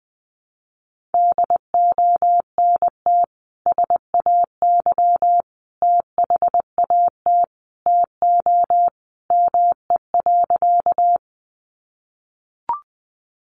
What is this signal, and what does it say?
Morse code, heard as a keyed tone, about 20 wpm, 700 Hz. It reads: DONT SAY THAT TO ME.
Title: Sentences from Top 300 Words 20wpm